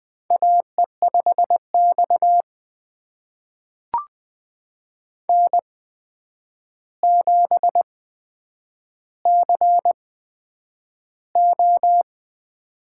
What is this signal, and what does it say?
Morse code, heard as a keyed tone, about 20 wpm, spaced as for 9 wpm, 700 Hz. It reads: AE5X N7CO
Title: USA Call Signs - 4 Characters - 8x Character Spacing - 20wpm